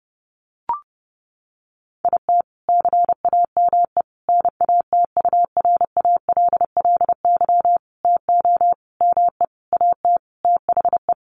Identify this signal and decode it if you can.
Morse code, heard as a keyed tone, about 30 wpm, 700 Hz. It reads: IT CAME NATURALLY TO ME AT THE
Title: Sentences from Top 1300 Words 30wpm